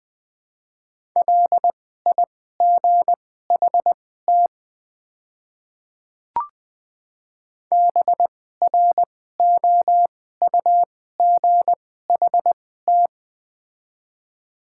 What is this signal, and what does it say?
Morse code, heard as a keyed tone, about 21 wpm, spaced as for 15 wpm, 700 Hz. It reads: LIGHT BROUGHT
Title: 2x Character Spacing - Top 500 Words - 20wpm